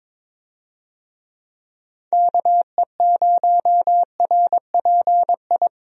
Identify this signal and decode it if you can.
Morse code, heard as a keyed tone, about 21 wpm, 700 Hz. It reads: KE0RPI
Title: USA Call Signs - 6 Characters - 22wpm